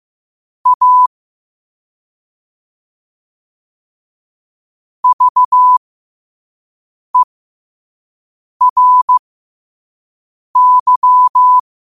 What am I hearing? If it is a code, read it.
Morse code, heard as a keyed tone, about 15 wpm, spaced as for 5 wpm, 1000 Hz. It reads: A VERY
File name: qotdfiles / 5wpm